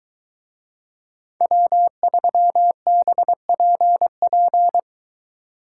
Morse code, heard as a keyed tone, about 20 wpm, 700 Hz. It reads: W3BPP